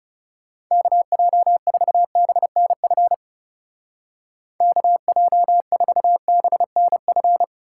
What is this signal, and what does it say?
Morse code, heard as a keyed tone, about 38 wpm, 700 Hz. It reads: KJ4BNF KJ4BNF